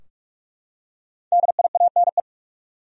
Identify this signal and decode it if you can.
Morse code, heard as a keyed tone, about 46 wpm, 700 Hz. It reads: DIANE
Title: USA Common Names - 45wpm